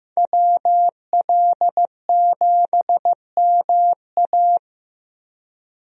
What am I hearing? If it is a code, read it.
Morse code, heard as a keyed tone, about 15 wpm, 700 Hz. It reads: WL7MA
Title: USA Call Signs - 5 Characters - 15wpm